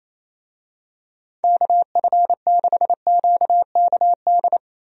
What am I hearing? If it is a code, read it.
Morse code, heard as a keyed tone, about 28 wpm, 700 Hz. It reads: KF6QKD
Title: USA Call Signs - 6 Characters - 28wpm